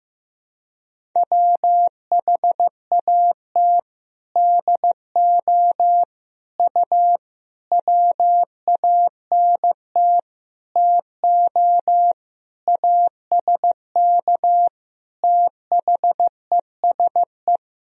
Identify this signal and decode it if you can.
Morse code, heard as a keyed tone, about 15 wpm, 700 Hz. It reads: WHAT DO U WANT TO ASK THESE
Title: Sentences from Top 200 Words and Abbreviations 15wpm